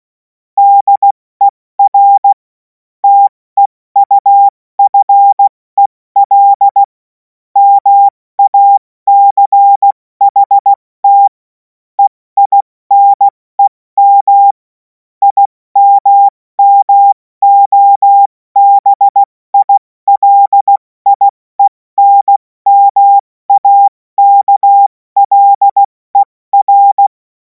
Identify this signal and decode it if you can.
Morse code, heard as a keyed tone, about 16 wpm, 800 Hz. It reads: DER TEUFEL MACHT EINEM IMMOBILIENMAKLER
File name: Morsekurs / Witze / Witz 30. Juni 2026 70 F80